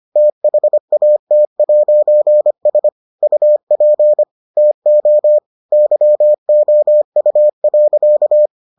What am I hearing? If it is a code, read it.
Morse code, heard as a keyed tone, about 25 wpm, 600 Hz. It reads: THAT'S UP TO YOU.